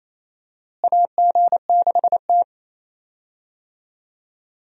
Morse code, heard as a keyed tone, about 28 wpm, 700 Hz. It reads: AG6T
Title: USA Call Signs - 4 Characters - Repeated 2x - 28wpm